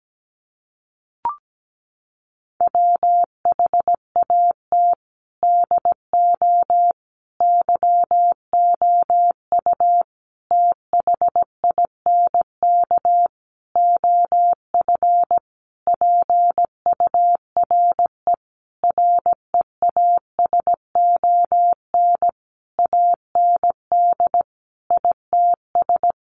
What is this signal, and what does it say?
Morse code, heard as a keyed tone, about 17 wpm, 700 Hz. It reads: WHAT DO YOU THINK OF PURE REASON AND ITS
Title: Sentences from Top 1500 Words 17wpm